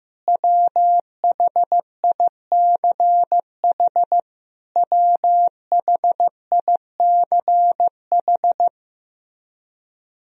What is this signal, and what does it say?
Morse code, heard as a keyed tone, about 15 wpm, 700 Hz. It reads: WHICH WHICH
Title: Top 100 Words Repeated 15wpm